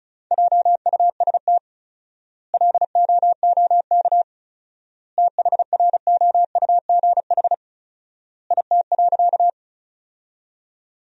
Morse code, heard as a keyed tone, about 36 wpm, 700 Hz. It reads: JUST LOOK THROUGH IT.
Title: 4x Word-Spacing - Sentences from Top 200 Words - 35wpm